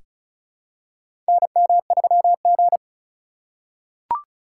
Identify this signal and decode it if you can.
Morse code, heard as a keyed tone, about 36 wpm, 700 Hz. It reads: NM3G
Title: USA Call Signs - 4 Characters - 4x Character Spacing - 35wpm